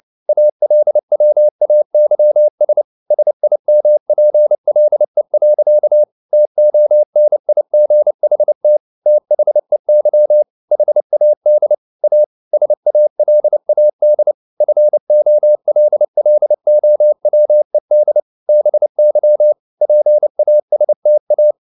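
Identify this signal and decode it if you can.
Morse code, heard as a keyed tone, about 29 wpm, 600 Hz. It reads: ALWAYS SIMPLE. TONIGHT THEY HAD A SALAD FOLLOWED BY PASTA
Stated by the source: Morse code at 29 wpm, sending Ken Follett's novel Never in English, from CW med Gnister.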